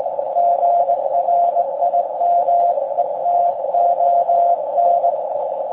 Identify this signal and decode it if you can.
Morse code, heard as a keyed tone, about 19 wpm, 690 Hz, with radio noise. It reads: GRIMETON
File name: forum / im / aq grimeton